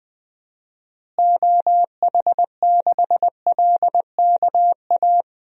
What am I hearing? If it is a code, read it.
Morse code, heard as a keyed tone, about 20 wpm, 700 Hz. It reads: OH6LKA
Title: DXCC-100 Call Signs - 6 Characters - Repeated 3x - 20wpm